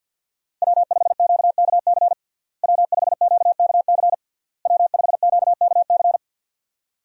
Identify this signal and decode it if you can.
Morse code, heard as a keyed tone, about 52 wpm, 700 Hz. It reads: W5QKC W5QKC W5QKC
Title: USA Call Signs - 5 Characters - Repeated 3x - 50wpm